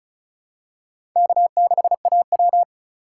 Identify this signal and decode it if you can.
Morse code, heard as a keyed tone, about 36 wpm, 700 Hz. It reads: K6AW
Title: USA Call Signs - 4x Character Spacing - 35wpm